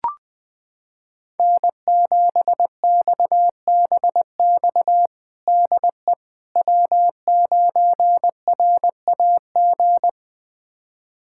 Morse code, heard as a keyed tone, about 20 wpm, 700 Hz. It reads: N7XBX DE W9RAG
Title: USA Callsign DE Callsign - 5 Characters - 20wpm